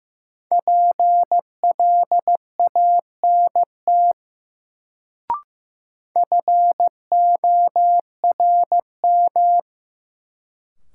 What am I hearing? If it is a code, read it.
Morse code, heard as a keyed tone, about 15 wpm, 700 Hz. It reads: PLANT FORM
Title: Sentences from Top 1000 Words - Review Part 1 - New Word Drill - 15wpm